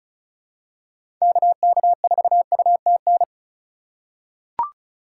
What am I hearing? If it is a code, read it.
Morse code, heard as a keyed tone, about 36 wpm, 700 Hz. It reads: KK4UTN
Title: USA Call Signs - 6 Characters - 2x Character Spacing - 35wpm